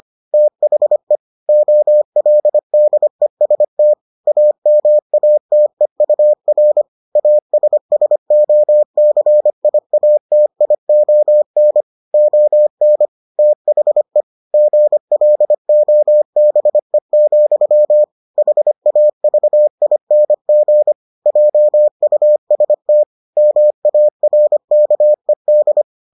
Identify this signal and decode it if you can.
Morse code, heard as a keyed tone, about 25 wpm, 600 Hz. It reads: THE OLDEST AMATEUR ASSOCIATION ON THE GLOBE, HAVING JUST MARKED